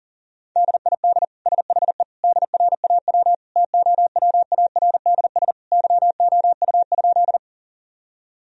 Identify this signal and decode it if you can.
Morse code, heard as a keyed tone, about 37 wpm, 700 Hz. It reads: DID SHE DRAW TOWARDS YOU?
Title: Sentences from Top 1000 Words - Review Part 2 - 40wpm